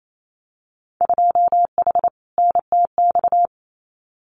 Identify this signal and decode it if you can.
Morse code, heard as a keyed tone, about 28 wpm, 700 Hz. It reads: 2H NTX
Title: Winter Field Day Exchanges Encore 28wpm